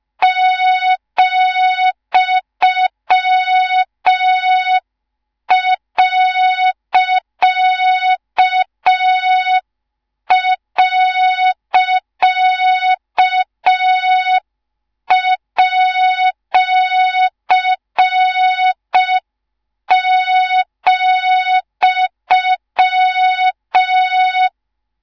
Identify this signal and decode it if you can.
Morse code, heard as a keyed tone, about 5 wpm, 3700 Hz, with noise. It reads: ,..@,